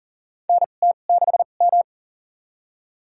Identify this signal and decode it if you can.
Morse code, heard as a keyed tone, about 40 wpm, spaced as for 29 wpm, 700 Hz. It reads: NT6M
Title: USA Call Signs - 2x Character Spacing - 40wpm